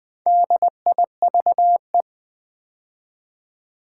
Morse code, heard as a keyed tone, about 20 wpm, 700 Hz. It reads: DIVE
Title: Build A Word - 2 Letters At A Time - 6-Letter Words - 20wpm